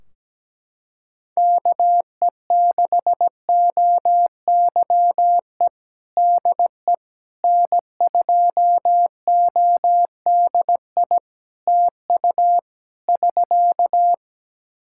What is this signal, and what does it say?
Morse code, heard as a keyed tone, about 17 wpm, 700 Hz. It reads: KE6OYE DE N2ODI TU <SK>
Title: K1USN SST QSO Practice 17wpm